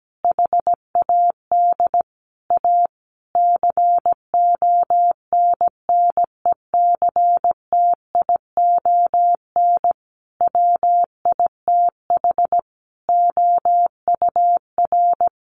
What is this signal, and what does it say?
Morse code, heard as a keyed tone, about 17 wpm, 700 Hz. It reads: HAD A CONNECTION WITH OUR
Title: Sentences from Top 1300 Words 17wpm